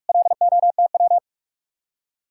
Morse code, heard as a keyed tone, about 59 wpm, spaced as for 52 wpm, 700 Hz. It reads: LOTW
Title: Speed-Racing Common QSO Phrases Encore - 45wpm